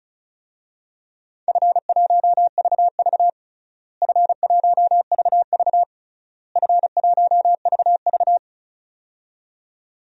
Morse code, heard as a keyed tone, about 34 wpm, 700 Hz. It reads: F1VV F1VV F1VV
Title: DXCC-100 Call Signs - 4 Characters - Repeated 3x - 35wpm